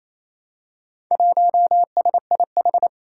Morse code, heard as a keyed tone, about 28 wpm, 700 Hz. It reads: 1SIH
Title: Four Challenging Letter-Number Combinations 28wpm